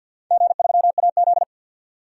Morse code, heard as a keyed tone, about 52 wpm, 700 Hz. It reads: G3AZ
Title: DXCC-100 Call Signs - 4 Characters - 50wpm